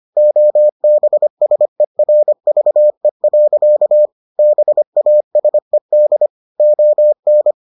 Morse code, heard as a keyed tone, about 25 wpm, 600 Hz. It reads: OBSERVE. BASED ON